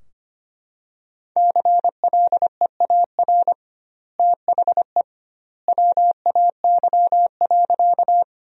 Morse code, heard as a keyed tone, about 25 wpm, 700 Hz. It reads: CLEAR THE WAY.
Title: 2x Word-Spacing - Sentences from Top 500 Words - 25wpm